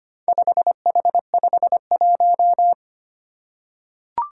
Morse code, heard as a keyed tone, about 25 wpm, 700 Hz. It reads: 5H51
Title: Four Challenging Letter-Number Combinations 25wpm